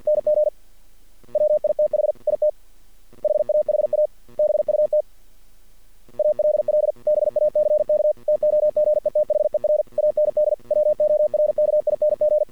Morse code, heard as a keyed tone, about 48 wpm, 600 Hz. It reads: NG WITH IT HAHA BUT ALL BACK TOGETHER AND WORKING
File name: archived / CRSnet